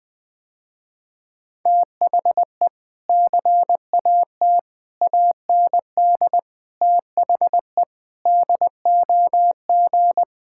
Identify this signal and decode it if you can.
Morse code, heard as a keyed tone, about 20 wpm, 700 Hz. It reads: THE CAT AND THE DOG
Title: Sentences with 2-Word Phrases - Spoken Beginning - 20wpm